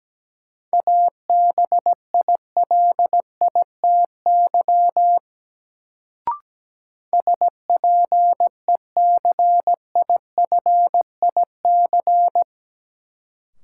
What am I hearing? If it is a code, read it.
Morse code, heard as a keyed tone, about 17 wpm, 700 Hz. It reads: ABILITY SPECIFIC
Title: Sentences from Top 1000 Words - Review Part 3 - New Word Drill - 17wpm